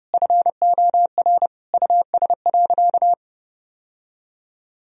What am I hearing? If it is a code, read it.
Morse code, heard as a keyed tone, about 32 wpm, 700 Hz. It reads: FOR US.
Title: Sentences from Top 1000 Words - Review Part 3 - 30wpm